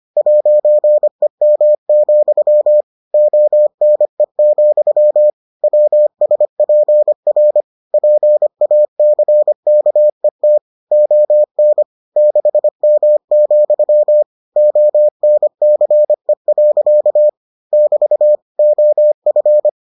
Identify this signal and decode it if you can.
Morse code, heard as a keyed tone, about 25 wpm, 600 Hz. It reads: 'EM, ONE, WSPR PACKET ON 6M, ONCE. <BT> OF